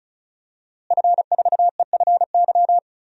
Morse code, heard as a keyed tone, about 36 wpm, 700 Hz. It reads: F4EFY
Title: DXCC-100 Call Signs - 5 Characters - Repeated 3x - 35wpm